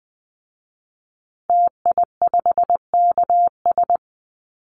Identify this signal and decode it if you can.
Morse code, heard as a keyed tone, about 19 wpm, 700 Hz. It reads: TI5KS
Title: Call Signs Repeated 2x - 20wpm